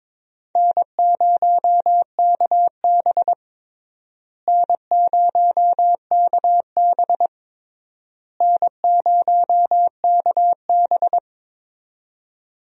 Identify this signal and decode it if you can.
Morse code, heard as a keyed tone, about 22 wpm, 700 Hz. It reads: N0KB N0KB N0KB
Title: USA Call Signs - 4 Characters - Repeated 3x - 22wpm